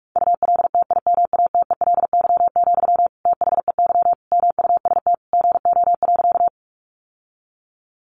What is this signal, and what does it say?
Morse code, heard as a keyed tone, about 44 wpm, 700 Hz. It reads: ULTIMATELY, THEY MUST GO.
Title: Sentences from Top 1200 Words 45wpm